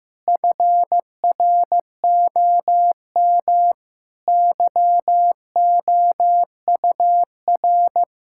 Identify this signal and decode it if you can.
Morse code, heard as a keyed tone, about 15 wpm, 700 Hz. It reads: FROM YOUR
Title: Sentences from Top 1600 Words 15wpm